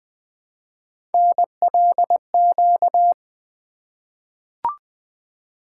Morse code, heard as a keyed tone, about 20 wpm, 700 Hz. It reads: NLQ
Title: Three Letter-Number Combinations - 20wpm